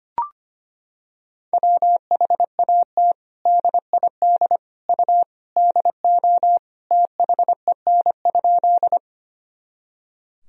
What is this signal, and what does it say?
Morse code, heard as a keyed tone, about 25 wpm, 700 Hz. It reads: WHAT DID U DO THEN?
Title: Sentences from Top 100 Words and Abbreviations 25wpm